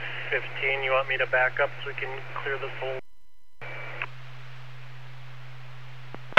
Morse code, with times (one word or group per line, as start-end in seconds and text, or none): none